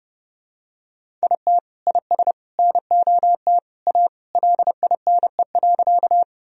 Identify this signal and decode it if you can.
Morse code, heard as a keyed tone, about 30 wpm, 700 Hz. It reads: IT IS NOT A LINE.
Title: Sentences from Top 1000 Words - Review Part 1 - 30wpm